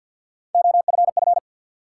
Morse code, heard as a keyed tone, about 53 wpm, 700 Hz. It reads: OFF
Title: Sets of 2 Words - New Word Drill - 50wpm